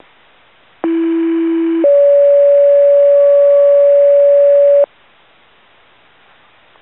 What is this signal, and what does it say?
Two-tone page, heard as a one-second tone, then a three-second tone, about 330 then 570 Hz. It is heard from 0.8 to 4.8 s.